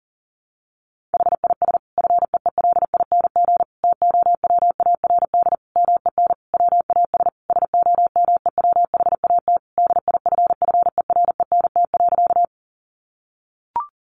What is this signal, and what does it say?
Morse code, heard as a keyed tone, about 40 wpm, 700 Hz. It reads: HIS FEELING TOWARD MEN WAS SOMEWHAT DIFFERENT.
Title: Sentences from Top 1300 Words 40wpm